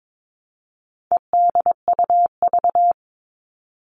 Morse code, heard as a keyed tone, about 22 wpm, 700 Hz. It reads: EDUV
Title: All Cut Numbers - 4-Digit 22wpm